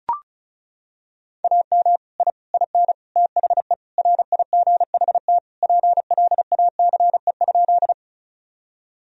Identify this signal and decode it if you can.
Morse code, heard as a keyed tone, about 36 wpm, 700 Hz. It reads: AM I IN THE RIGHT PLACE?
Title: Sentences from Top 1000 Words - Review Part 2 - 35wpm